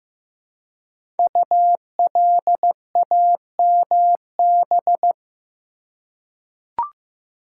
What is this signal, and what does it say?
Morse code, heard as a keyed tone, about 15 wpm, 700 Hz. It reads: ULAMB
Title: Code Groups - 5 Letters - 15wpm